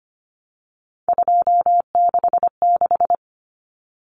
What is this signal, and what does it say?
Morse code, heard as a keyed tone, about 25 wpm, 700 Hz. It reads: 266
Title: Common Cut Numbers - 3-Digit 25wpm